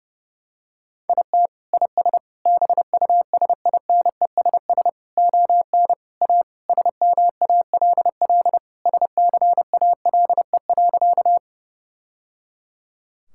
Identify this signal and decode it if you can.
Morse code, heard as a keyed tone, about 29 wpm, 700 Hz. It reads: IT IS BUSINESS ON A SMALL SCALE.
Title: Sentences from Top 1000 Words - Review Part 5 - 30wpm